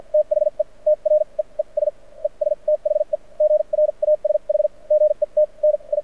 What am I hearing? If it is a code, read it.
Morse code, heard as a keyed tone, about 56 wpm, spaced as for 40 wpm, 600 Hz. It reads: THE TREES ES THE GRASH GETNI
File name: archived / CRSnet